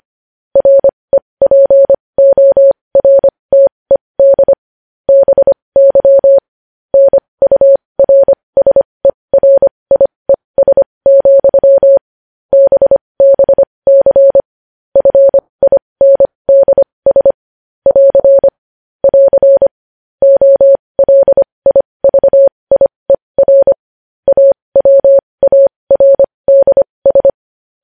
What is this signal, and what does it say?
Morse code, heard as a keyed tone, about 25 wpm, spaced as for 20 wpm, 550 Hz. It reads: REPORTED BY NURSERIES, BBC FINDS <AR> <AR> OLIVIER AWARDS
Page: news headlines in morse code at 20 wpm